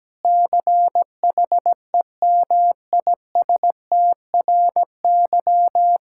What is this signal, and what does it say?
Morse code, heard as a keyed tone, about 17 wpm, 700 Hz. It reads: CHEMISTRY
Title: Sentences from Top 1000 Words - Review Part 6 - 17wpm